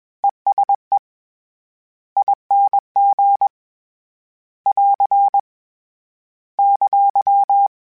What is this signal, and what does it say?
Morse code, heard as a keyed tone, about 21 wpm, 800 Hz. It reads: ESE ING <AR> !
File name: Short Runs / Short Runs-mp3s / ESE ING @20